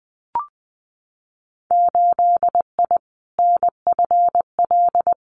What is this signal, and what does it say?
Morse code, heard as a keyed tone, about 19 wpm, 700 Hz. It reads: 8I NFL
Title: Winter Field Day Exchanges Encore 20wpm